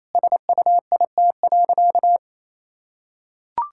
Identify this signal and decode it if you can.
Morse code, heard as a keyed tone, about 28 wpm, 700 Hz. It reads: SUIT.
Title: Sentences from Top 1600 Words 28wpm